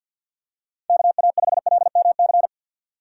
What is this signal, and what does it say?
Morse code, heard as a keyed tone, about 52 wpm, 700 Hz. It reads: KA5LMC
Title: USA Call Signs - 6 Characters - 6x Character Spacing - 50wpm